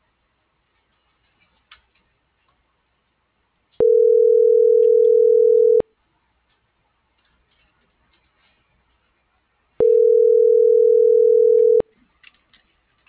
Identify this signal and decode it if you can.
Ringback tone (North America), 440 + 480 Hz.